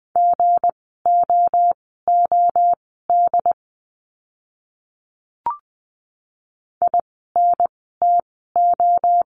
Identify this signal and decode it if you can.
Morse code, heard as a keyed tone, about 20 wpm, spaced as for 14 wpm, 700 Hz. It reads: GOOD INTO
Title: 244 Lesson 39f - Letter z - Words 20wpm